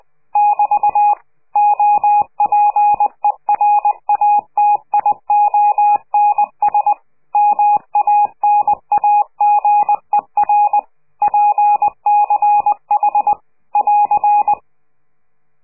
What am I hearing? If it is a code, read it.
Morse code, heard as a keyed tone, about 20 wpm, 830 Hz. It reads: <BT> OPERATIONS MANAGER PCH <AR>
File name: CQ de PCH laatste bericht farewell